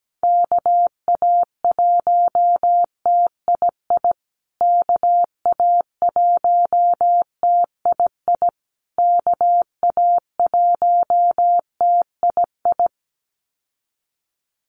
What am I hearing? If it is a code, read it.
Morse code, heard as a keyed tone, about 17 wpm, 700 Hz. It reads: KA1TII KA1TII KA1TII
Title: USA Call Signs Repeated 3x 17wpm